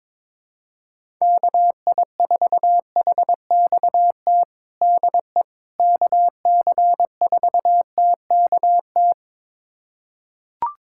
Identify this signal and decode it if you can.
Morse code, heard as a keyed tone, about 22 wpm, 700 Hz. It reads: KI4HXT DE KC4TKT
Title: USA Callsign DE Callsign - 6 Characters - 22wpm